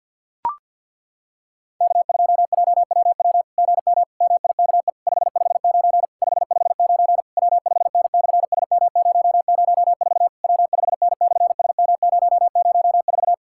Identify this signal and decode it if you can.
Morse code, heard as a keyed tone, about 50 wpm, 700 Hz. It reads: K2JWW GM MIKE 559 559 W5N/SM004 W5N/SM004
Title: SOTA QSO Practice 50wpm